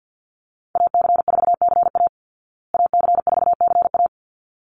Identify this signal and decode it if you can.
Morse code, heard as a keyed tone, about 49 wpm, 700 Hz. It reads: AC4CA AC4CA
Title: Call Signs Repeated 3x - 50wpm